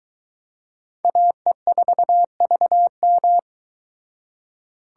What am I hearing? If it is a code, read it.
Morse code, heard as a keyed tone, about 22 wpm, 700 Hz. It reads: AE4VM